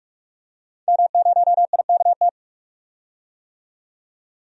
Morse code, heard as a keyed tone, about 45 wpm, 700 Hz. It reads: M0IKT
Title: DXCC-100 Call Signs - 5 Characters - 45wpm